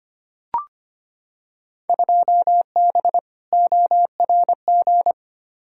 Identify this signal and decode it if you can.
Morse code, heard as a keyed tone, about 25 wpm, 700 Hz. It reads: 2B ORG
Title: Field Day Exchanges Encore - 25wpm